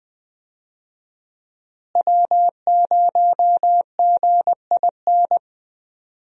Morse code, heard as a keyed tone, about 21 wpm, 700 Hz. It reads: W0GIN
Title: USA Call Signs - 5 Characters - Repeated 2x - 20wpm